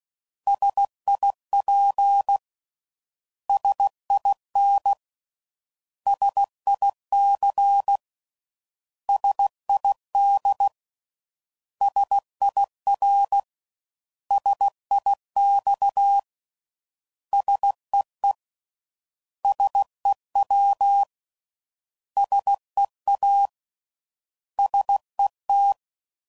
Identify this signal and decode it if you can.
Morse code, heard as a keyed tone, about 16 wpm, 800 Hz. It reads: SIP SIN SIC SID SIR SIX SEE SEW SEA SET
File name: Prefixes / By First Letter / Ss